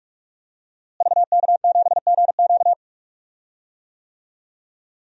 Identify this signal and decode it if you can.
Morse code, heard as a keyed tone, about 41 wpm, 700 Hz. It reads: VK7GQ
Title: DXCC-100 Call Signs - 5 Characters - Repeated 2x - 45wpm